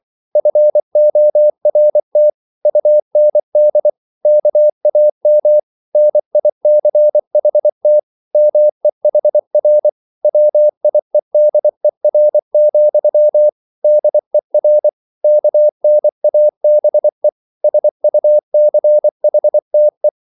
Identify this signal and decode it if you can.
Morse code, heard as a keyed tone, about 24 wpm, 600 Hz. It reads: FORT UND KAM NICHT MEHR WIEDER, DER KNABE SUCHTE
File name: Liebespaar2 24wpm